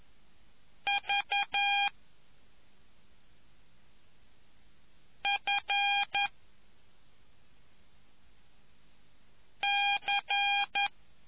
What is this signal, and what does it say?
Morse code, heard as a keyed tone, about 11 wpm, 2400 Hz. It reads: V F C